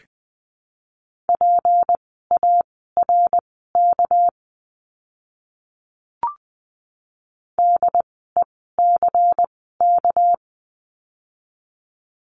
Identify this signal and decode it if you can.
Morse code, heard as a keyed tone, about 19 wpm, spaced as for 14 wpm, 700 Hz. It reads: PARK DECK
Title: 199 Lesson 32f - Letter k - Words 20wpm